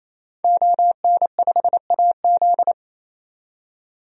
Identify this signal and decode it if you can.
Morse code, heard as a keyed tone, about 28 wpm, 700 Hz. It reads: ON5AZ
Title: DXCC-100 Call Signs - 5 Characters - 28wpm